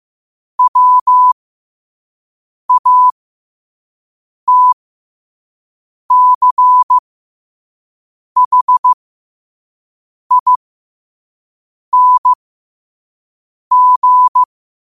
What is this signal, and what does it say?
Morse code, heard as a keyed tone, about 15 wpm, spaced as for 5 wpm, 1000 Hz. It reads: WATCHING
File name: qotdfiles / 5wpm